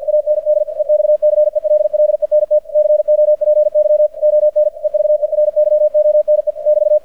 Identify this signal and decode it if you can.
Morse code, heard as a keyed tone, about 50 wpm, 600 Hz. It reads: AND EXCELLENT COPY ON EVERYONE <BT>
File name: archived / CRSnet